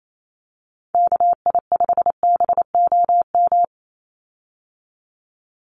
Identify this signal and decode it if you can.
Morse code, heard as a keyed tone, about 28 wpm, 700 Hz. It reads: KI5BOM
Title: USA Call Signs 28wpm